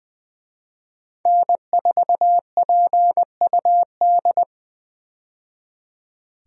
Morse code, heard as a keyed tone, about 20 wpm, 700 Hz. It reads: N4PUD
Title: USA Call Signs - 5 Characters - 20wpm